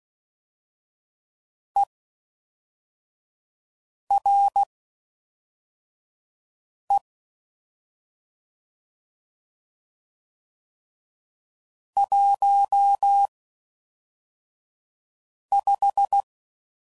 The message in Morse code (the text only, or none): ERE 15